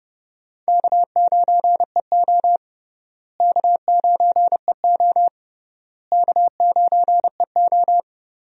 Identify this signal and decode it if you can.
Morse code, heard as a keyed tone, about 32 wpm, 700 Hz. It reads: K9EO K9EO K9EO
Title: USA Call Signs - 4 Characters - Repeated 3x - 30wpm